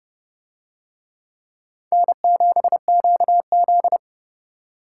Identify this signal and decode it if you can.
Morse code, heard as a keyed tone, about 29 wpm, 700 Hz. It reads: N7QZ